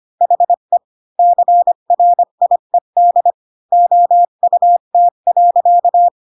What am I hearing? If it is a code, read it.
Morse code, heard as a keyed tone, about 25 wpm, 700 Hz. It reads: HE CRIED OUT.
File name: Pinocchio / Pinocchio ch3 20 25 ik1hge